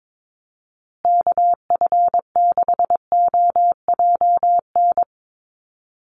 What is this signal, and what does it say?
Morse code, heard as a keyed tone, about 22 wpm, 700 Hz. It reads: KF6OJN
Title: USA Call Signs Repeated 2x 22wpm